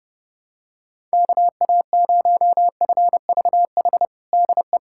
Morse code, heard as a keyed tone, about 32 wpm, 700 Hz. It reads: KA0FVH DE
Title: USA Callsign DE Callsign - 6 Characters - 30wpm